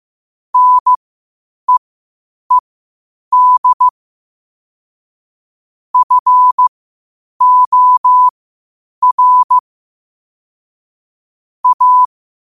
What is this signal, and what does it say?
Morse code, heard as a keyed tone, about 15 wpm, spaced as for 8 wpm, 1000 Hz. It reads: NEED FOR A
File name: qotdfiles / 8wpm